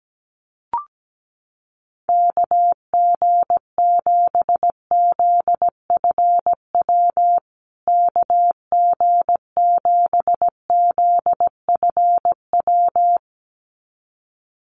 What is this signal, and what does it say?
Morse code, heard as a keyed tone, about 17 wpm, 700 Hz. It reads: KG7ZFW KG7ZFW
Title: USA Call Signs Repeated 2x 17wpm